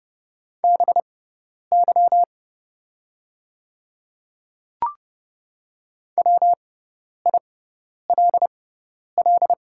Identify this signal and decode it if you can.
Morse code, heard as a keyed tone, about 30 wpm, spaced as for 10 wpm, 700 Hz. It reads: BY WILL
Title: 6x Character Spacing - Top 100 Words - 30wpm